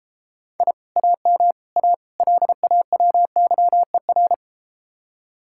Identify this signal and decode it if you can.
Morse code, heard as a keyed tone, about 32 wpm, 700 Hz. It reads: I AM A LAWYER